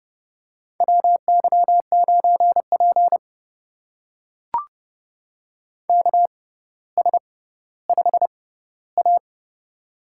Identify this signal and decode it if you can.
Morse code, heard as a keyed tone, about 28 wpm, spaced as for 15 wpm, 690 Hz. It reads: WY9P KS5A
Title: USA Call Signs - 4 Characters - 6x Character Spacing - 30wpm